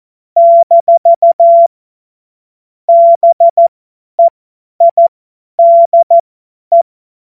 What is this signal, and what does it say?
Morse code, heard as a keyed tone, about 14 wpm, spaced as for 10 wpm, 680 Hz. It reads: - BEIDE
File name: Morsekurs / Witze / Witz 09. November 2021, 50 F70